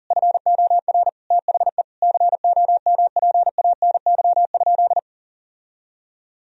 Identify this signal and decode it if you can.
Morse code, heard as a keyed tone, about 40 wpm, 700 Hz. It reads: FOR THE COMPANY?
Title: Sentences from Top 1000 Words - Review Part 3 - 40wpm